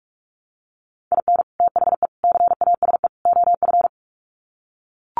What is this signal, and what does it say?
Morse code, heard as a keyed tone, about 44 wpm, 700 Hz. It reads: IN THE CASE OF
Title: Sets of 4 Words 45wpm